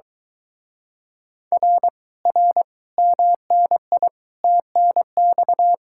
Morse code, heard as a keyed tone, about 22 wpm, 700 Hz. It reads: R R MNI TNX